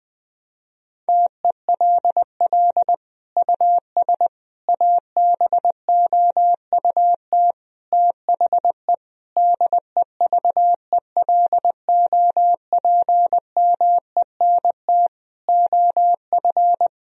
Morse code, heard as a keyed tone, about 21 wpm, 700 Hz. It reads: TELL US ABOUT THE DEVELOPMENT OF
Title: Sentences from Top 1000 Words - Review Part 4 - 20wpm